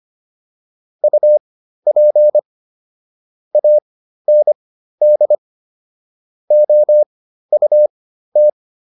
Morse code, heard as a keyed tone, about 25 wpm, spaced as for 13 wpm, 600 Hz. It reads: UP AND OUT